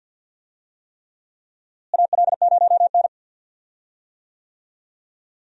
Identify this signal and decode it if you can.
Morse code, heard as a keyed tone, about 52 wpm, 700 Hz. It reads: AL0N
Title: USA Call Signs - 4 Characters - Repeated 3x - 50wpm